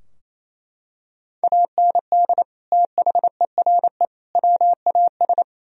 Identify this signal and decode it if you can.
Morse code, heard as a keyed tone, about 28 wpm, 700 Hz. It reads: AND THERE WAS
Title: Sentences from Top 1000 Words - Review Part 3 - 28wpm